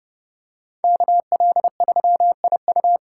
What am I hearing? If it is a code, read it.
Morse code, heard as a keyed tone, about 30 wpm, 700 Hz. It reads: KL3IU